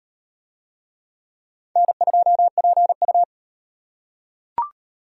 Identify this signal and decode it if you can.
Morse code, heard as a keyed tone, about 38 wpm, 700 Hz. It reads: N2PU